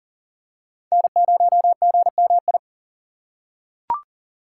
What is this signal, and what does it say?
Morse code, heard as a keyed tone, about 38 wpm, 700 Hz. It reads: N0GMI
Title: USA Call Signs - 6x Character Spacing - 40wpm